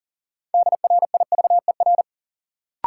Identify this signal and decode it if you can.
Morse code, heard as a keyed tone, about 40 wpm, 700 Hz. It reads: DRIVER
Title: Sentences from Top 1000 Words - New Word Drill - 40wpm